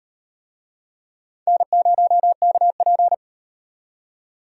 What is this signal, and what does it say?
Morse code, heard as a keyed tone, about 38 wpm, 700 Hz. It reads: N0KP